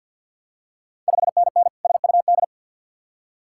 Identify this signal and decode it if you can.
Morse code, heard as a keyed tone, about 52 wpm, 690 Hz. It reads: 5NN SUD
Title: Ontario QSO Party Exchange 50wpm